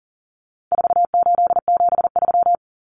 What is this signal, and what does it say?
Morse code, heard as a keyed tone, about 40 wpm, 700 Hz. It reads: 4873
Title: Common Cut Numbers - 4-Digit 40wpm